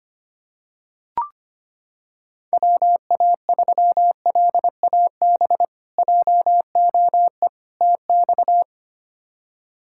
Morse code, heard as a keyed tone, about 25 wpm, 700 Hz. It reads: WA3LAB JOE TX
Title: K1USN SST QSO Practice - 25wpm